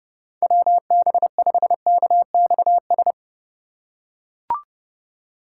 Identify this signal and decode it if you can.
Morse code, heard as a keyed tone, about 29 wpm, 700 Hz. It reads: WB5KXS